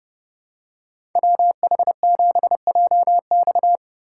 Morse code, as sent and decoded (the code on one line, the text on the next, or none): .-- .... --... .--- -..-
WH7JX